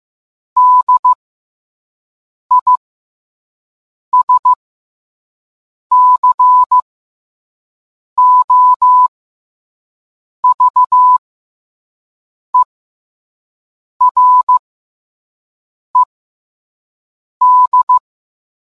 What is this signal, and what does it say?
Morse code, heard as a keyed tone, about 15 wpm, spaced as for 5 wpm, 1000 Hz. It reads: DISCOVERED